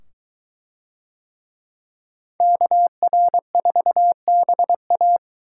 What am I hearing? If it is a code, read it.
Morse code, heard as a keyed tone, about 22 wpm, 700 Hz. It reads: KR4BA